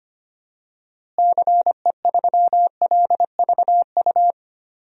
Morse code, heard as a keyed tone, about 25 wpm, 700 Hz. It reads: CE3LVU
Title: DXCC-100 Call Signs - 6 Characters - Repeated 2x - 25wpm